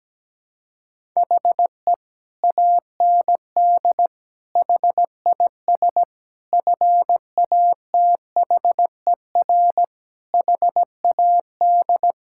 Morse code, heard as a keyed tone, about 17 wpm, 700 Hz. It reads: HE AND HIS FATHER HAD
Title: Sentences from Top 1600 Words 17wpm